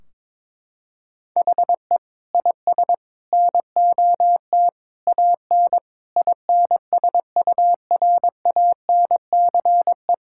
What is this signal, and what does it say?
Morse code, heard as a keyed tone, about 22 wpm, 700 Hz. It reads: HE IS NOT AN INSURANCE
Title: Sentences from Top 1600 Words 22wpm